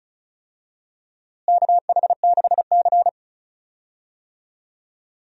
Morse code, heard as a keyed tone, about 34 wpm, 700 Hz. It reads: KH6C
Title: DXCC-100 Call Signs - 4 Characters - Repeated 3x - 35wpm